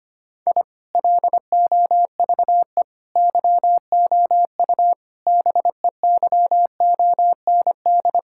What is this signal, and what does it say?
Morse code, heard as a keyed tone, about 25 wpm, 700 Hz. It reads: I LOVE YOU BEYOND
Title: Sentences from Top 1000 Words - Review Part 6 - 25wpm